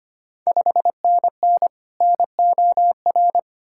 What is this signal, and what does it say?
Morse code, heard as a keyed tone, about 25 wpm, 700 Hz. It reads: 5NN NOR
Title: Ontario QSO Party Exchange 25wpm